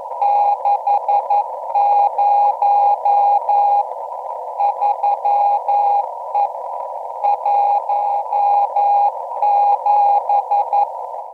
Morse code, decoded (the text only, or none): 60 3E 17